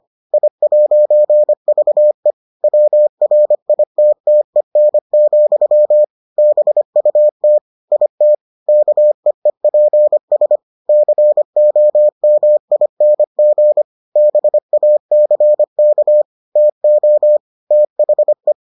Morse code, heard as a keyed tone, about 25 wpm, 600 Hz. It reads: I'VE WRITTEN, BUT IT KEEPS COMING BACK TO THE